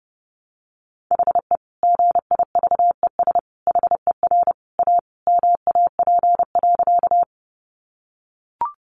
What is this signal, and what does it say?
Morse code, heard as a keyed tone, about 30 wpm, 700 Hz. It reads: HE GIVES HER A MAP.
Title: Additional Easy Phrases 30wpm